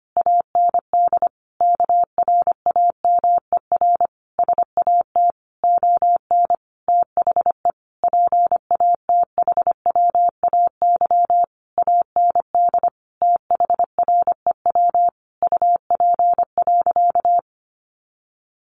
Morse code, heard as a keyed tone, about 25 wpm, 700 Hz. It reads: AND KRAMER SAT ON THE PATHWAY AND THREW UP.
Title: A Sound Of Thunder 25wpm